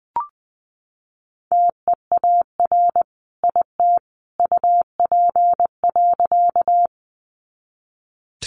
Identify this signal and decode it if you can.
Morse code, heard as a keyed tone, about 20 wpm, 700 Hz. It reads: TEAR IT UP.
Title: A Sound Of Thunder 20wpm